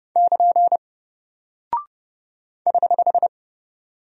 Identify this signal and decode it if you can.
Morse code, heard as a keyed tone, about 29 wpm, 700 Hz. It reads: <KN> <HH>
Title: Single Character (Letter:Number:Punctuation:Prosign) Alt 30wpm